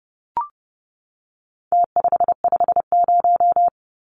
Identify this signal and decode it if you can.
Morse code, heard as a keyed tone, about 30 wpm, 700 Hz. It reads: T550
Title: Numbers - Contest Style - 4-Digit 30wpm